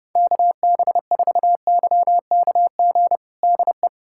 Morse code, heard as a keyed tone, about 30 wpm, 700 Hz. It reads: KB4YKG DE
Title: USA Callsign DE Callsign 30wpm